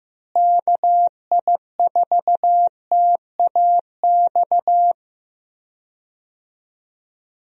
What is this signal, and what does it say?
Morse code, heard as a keyed tone, about 15 wpm, 700 Hz. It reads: KI4TAX
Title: USA Call Signs - 6 Characters - Repeated 3x - 15wpm